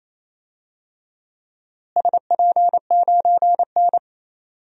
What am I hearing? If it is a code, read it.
Morse code, heard as a keyed tone, about 28 wpm, 700 Hz. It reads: SP9N
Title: DXCC-100 Call Signs - 4 Characters - Repeated 2x - 28wpm